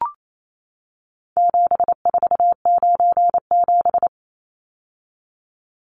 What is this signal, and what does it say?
Morse code, heard as a keyed tone, about 28 wpm, 700 Hz. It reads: 7497
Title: Numbers - 4-Digit 28wpm